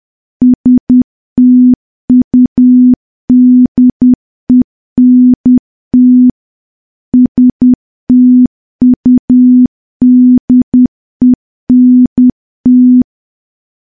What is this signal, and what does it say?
Morse code, heard as a keyed tone, about 10 wpm, 250 Hz. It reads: STUDENT STUDENT
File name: morse / morse-training-10WPM